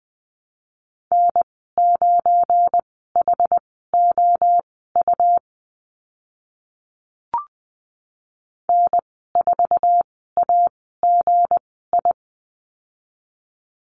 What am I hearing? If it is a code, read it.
Morse code, heard as a keyed tone, about 20 wpm, spaced as for 14 wpm, 700 Hz. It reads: N9HOU N4AGI
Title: 193 Lesson 31g - Letter v - Call Signs 20wpm